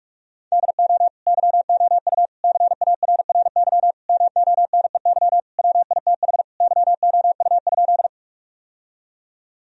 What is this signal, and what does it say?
Morse code, heard as a keyed tone, about 46 wpm, 700 Hz. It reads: DO YOU CARRY MONEY WITH YOU?
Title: Sentences from Top 1000 Words - Review Part 1 - 45wpm